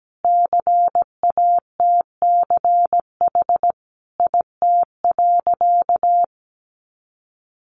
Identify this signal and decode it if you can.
Morse code, heard as a keyed tone, about 17 wpm, 700 Hz. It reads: CATCH IT.
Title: Additional Easy Phrases 17wpm